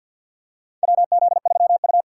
Morse code, heard as a keyed tone, about 52 wpm, 700 Hz. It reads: WZ3U
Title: USA Call Signs - 4 Characters - 3x Character Spacing - 50wpm